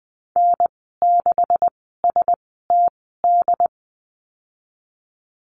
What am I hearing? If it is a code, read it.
Morse code, heard as a keyed tone, about 19 wpm, spaced as for 14 wpm, 700 Hz. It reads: N6STD
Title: 167 Lesson 27g - Number 7 - Call Signs 20wpm